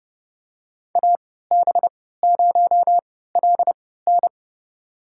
Morse code, heard as a keyed tone, about 30 wpm, spaced as for 17 wpm, 700 Hz. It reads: AB0LN
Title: USA Call Signs - 5 Characters - 3x Character Spacing - 30wpm